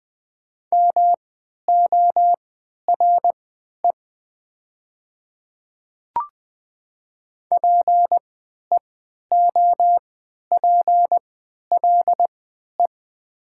Morse code, heard as a keyed tone, about 20 wpm, spaced as for 11 wpm, 700 Hz. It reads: MORE PEOPLE
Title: 3x Character Spacing - Top 200 Words - 20wpm